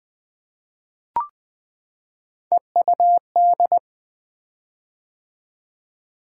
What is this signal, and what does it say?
Morse code, heard as a keyed tone, about 20 wpm, 700 Hz. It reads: EUD
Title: Three Letter-Number Combinations - 20wpm